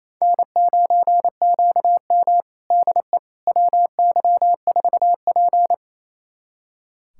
Morse code, heard as a keyed tone, about 27 wpm, 690 Hz. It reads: N9QM DE WY4P
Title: USA Callsign DE Callsign - 4 Characters - 28wpm